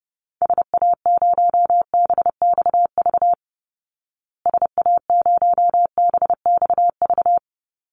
Morse code, heard as a keyed tone, about 30 wpm, 700 Hz. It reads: SA0BXV SA0BXV
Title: Call Signs Repeated 3x - 30wpm